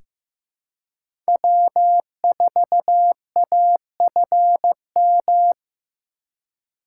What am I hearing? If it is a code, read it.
Morse code, heard as a keyed tone, about 15 wpm, 700 Hz. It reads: W4AFM
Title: USA Call Signs - 3x Character Spacing - 15wpm